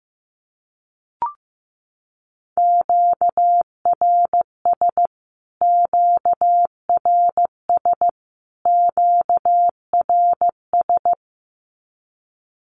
Morse code, heard as a keyed tone, about 15 wpm, 700 Hz. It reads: QRS QRS QRS
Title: Q-Codes Repeated 3x 15wpm